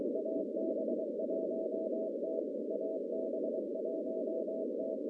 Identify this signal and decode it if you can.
Morse code, heard as a keyed tone, about 23 wpm, 640 Hz, with noise. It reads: AB1T AB1T
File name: contests / CW-SS-k5tr- -1100